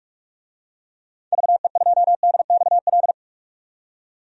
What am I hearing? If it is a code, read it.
Morse code, heard as a keyed tone, about 46 wpm, 700 Hz. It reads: VE2DXL
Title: DXCC-100 Call Signs - 6 Characters - 45wpm